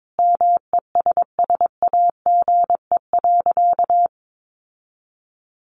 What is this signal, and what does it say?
Morse code, heard as a keyed tone, about 22 wpm, 700 Hz. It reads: MESSAGE.
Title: Sentences from Top 1100 Words 22wpm